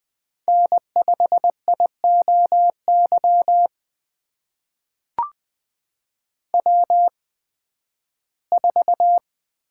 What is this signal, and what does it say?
Morse code, heard as a keyed tone, about 20 wpm, spaced as for 13 wpm, 700 Hz. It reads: N5IOY W4
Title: USA Call Signs - 5 Characters - 8x Character Spacing - 20wpm